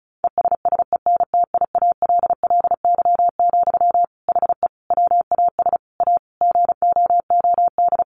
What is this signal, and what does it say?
Morse code, heard as a keyed tone, about 34 wpm, 700 Hz. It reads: ESSENTIALLY, HE WAS A GOOD
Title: Sentences from Top 1300 Words 35wpm